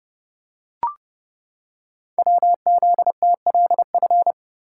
Morse code, heard as a keyed tone, about 30 wpm, 700 Hz. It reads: WZTLF
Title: Code Groups - 5 Letters - 30wpm